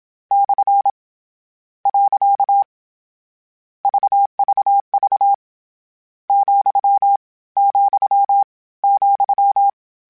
Morse code, heard as a keyed tone, about 27 wpm, 800 Hz. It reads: / . VVV , , ,
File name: Misc / Misc-mp3s / Punct @25